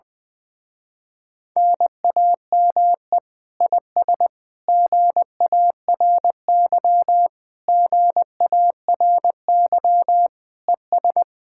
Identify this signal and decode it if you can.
Morse code, heard as a keyed tone, about 20 wpm, 700 Hz. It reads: NAME IS GARY GARY ES
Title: Speed-Racing Common QSO Phrases Encore - 17wpm